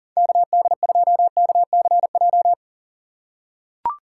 Morse code, heard as a keyed tone, about 40 wpm, 700 Hz. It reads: KD2KCJ
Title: USA Call Signs - 6 Characters - 3x Character Spacing - 40wpm